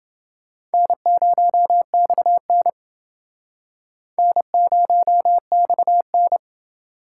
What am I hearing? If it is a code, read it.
Morse code, heard as a keyed tone, about 29 wpm, 700 Hz. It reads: N0XN N0XN